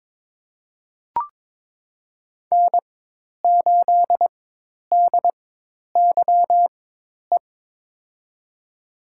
Morse code, heard as a keyed tone, about 21 wpm, spaced as for 10 wpm, 700 Hz. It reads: N8DYE
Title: USA Call Signs - 5 Characters - 4x Character Spacing - 22wpm